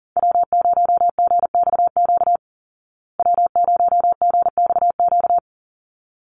Morse code, heard as a keyed tone, about 40 wpm, 700 Hz. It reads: W0GXQ W0GXQ
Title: Call Signs Repeated 3x - 40wpm